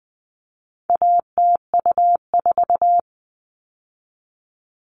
Morse code, heard as a keyed tone, about 20 wpm, 700 Hz. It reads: ATU4
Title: All Cut Numbers - 4-Digit 20wpm